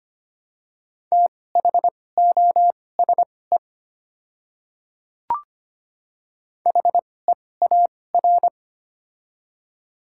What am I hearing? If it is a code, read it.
Morse code, heard as a keyed tone, about 25 wpm, spaced as for 18 wpm, 700 Hz. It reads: THOSE HEAR
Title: 2x Character Spacing - Top 300 Words - 25wpm